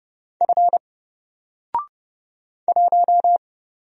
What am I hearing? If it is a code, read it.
Morse code, heard as a keyed tone, about 30 wpm, 700 Hz. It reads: F 1
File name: Single-Letter-Number-30wpm